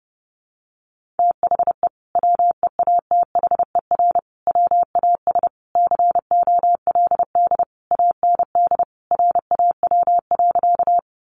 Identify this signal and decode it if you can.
Morse code, heard as a keyed tone, about 30 wpm, 700 Hz. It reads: THE WEATHER WAS COLD AND RAW.
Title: Sentences from Top 1500 Words 30wpm